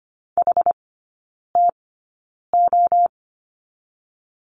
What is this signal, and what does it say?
Morse code, heard as a keyed tone, about 25 wpm, spaced as for 9 wpm, 700 Hz. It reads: HTO
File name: Lesson 17e - Letter u - Three Characters 25wpm